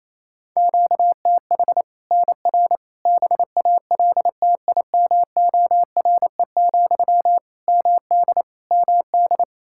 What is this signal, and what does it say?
Morse code, heard as a keyed tone, about 28 wpm, 700 Hz. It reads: QTH NR BALTIMORE, MD MD
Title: Practice QSO - Exchange 1 - Beginning - 28wpm